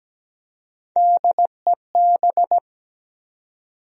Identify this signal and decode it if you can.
Morse code, heard as a keyed tone, about 17 wpm, 700 Hz. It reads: DEB
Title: USA Common Names - Max 3 Letters - 17wpm